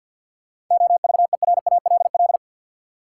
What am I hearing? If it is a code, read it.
Morse code, heard as a keyed tone, about 49 wpm, 700 Hz. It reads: OVERALL
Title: Sentences from Top 1000 Words - Review Part 7 - New Word Drill - 50wpm